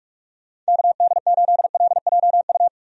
Morse code, heard as a keyed tone, about 42 wpm, 700 Hz. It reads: KD8LJU
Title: USA Call Signs - 6 Characters - 45wpm